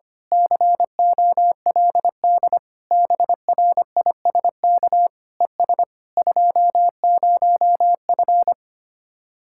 Morse code, heard as a keyed tone, about 24 wpm, 700 Hz. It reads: COLD BRISK ES 20F
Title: Speed-Racing Common QSO Phrases - 25wpm